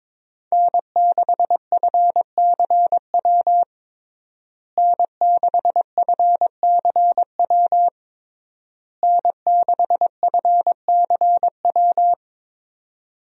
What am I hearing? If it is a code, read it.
Morse code, heard as a keyed tone, about 22 wpm, 700 Hz. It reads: N6FCW N6FCW N6FCW
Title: USA Call Signs - 5 Characters - Repeated 3x - 22wpm